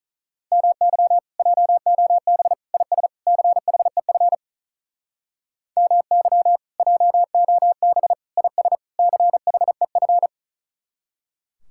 Morse code, heard as a keyed tone, about 44 wpm, 700 Hz. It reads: MY JOB IS CHEF MY JOB IS CHEF